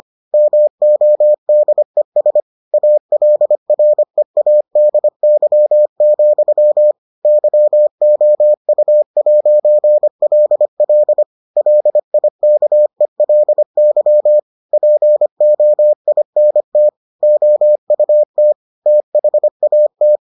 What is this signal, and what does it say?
Morse code, heard as a keyed tone, about 25 wpm, 600 Hz. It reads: MODES ALREADY, YOU'LL LIKELY POINT OUT THAT